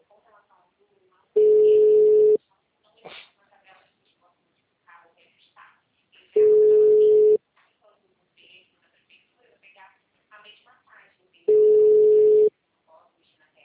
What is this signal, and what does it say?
Ringback tone (Europe, 425 Hz), with speech.